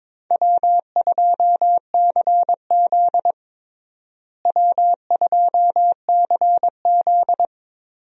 Morse code, heard as a keyed tone, about 22 wpm, 700 Hz. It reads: W2CZ W2CZ
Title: USA Call Signs - 4 Characters - Repeated 2x - 22wpm